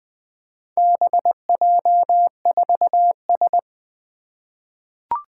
Morse code, heard as a keyed tone, about 20 wpm, 700 Hz. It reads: BJ4S
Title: Four Challenging Letter-Number Combinations 20wpm